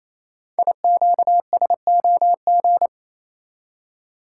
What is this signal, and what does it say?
Morse code, heard as a keyed tone, about 29 wpm, 700 Hz. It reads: IQSOG
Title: Code Groups - 5 Letters - 28wpm